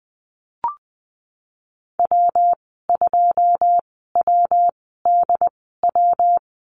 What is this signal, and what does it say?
Morse code, heard as a keyed tone, about 20 wpm, spaced as for 15 wpm, 700 Hz. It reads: W2WDW
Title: 122 Lesson 20g - Number 3 - Call Signs 20wpm